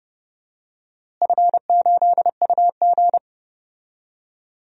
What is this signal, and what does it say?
Morse code, heard as a keyed tone, about 30 wpm, 700 Hz. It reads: F8UG
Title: DXCC-100 Call Signs - 4 Characters - Repeated 2x - 30wpm